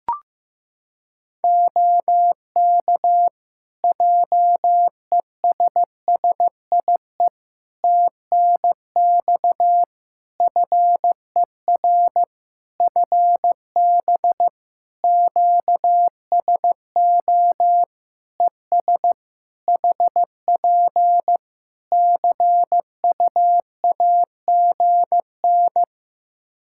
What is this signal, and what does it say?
Morse code, heard as a keyed tone, about 15 wpm, 700 Hz. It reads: OK JESSIE TNX FER FB QSO ES HP CUAGN
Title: Practice QSO - Exchange 4 - Ending - 15wpm